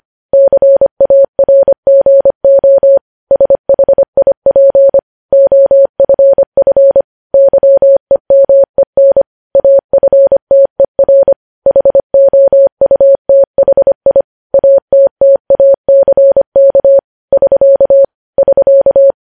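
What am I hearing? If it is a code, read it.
Morse code, heard as a keyed tone, about 25 wpm, 550 Hz. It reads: CARGO SHIP OFF YEMEN AFTER HOUTHI ATTACK <SK> <SK>
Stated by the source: Morse code news headlines at 25 WPM.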